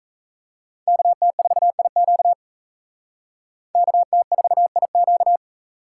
Morse code, heard as a keyed tone, about 36 wpm, 700 Hz. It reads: KT4IQ KT4IQ